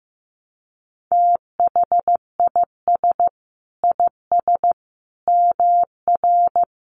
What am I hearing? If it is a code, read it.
Morse code, heard as a keyed tone, about 15 wpm, 700 Hz. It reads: THIS IS MR
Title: A Sound Of Thunder 15wpm